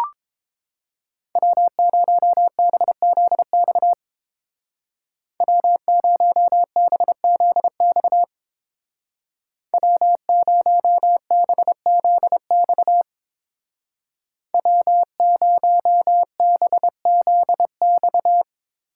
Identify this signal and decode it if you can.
Morse code, heard as a keyed tone, about 27 wpm, 700 Hz. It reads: W0BZX W0BZX W0BZX W0BZX
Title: Speed Racing - USA 1x3 Call Signs 22wpm